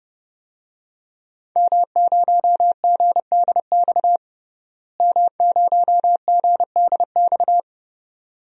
Morse code, heard as a keyed tone, about 30 wpm, 700 Hz. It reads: M0GDX M0GDX
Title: DXCC-100 Call Signs - 5 Characters - Repeated 2x - 30wpm